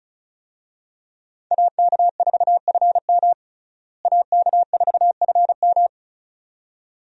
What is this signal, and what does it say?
Morse code, heard as a keyed tone, about 36 wpm, 700 Hz. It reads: AK4FM AK4FM
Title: USA Call Signs - 5 Characters - Repeated 2x - 35wpm